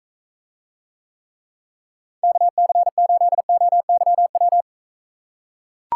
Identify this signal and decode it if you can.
Morse code, heard as a keyed tone, about 43 wpm, 700 Hz. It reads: KC8OYW